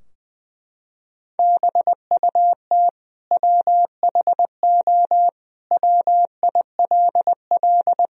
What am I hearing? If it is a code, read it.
Morse code, heard as a keyed tone, about 20 wpm, 700 Hz. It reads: BUT WHO WILL
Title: Sentences from Top 1000 Words - Review Part 2 - 20wpm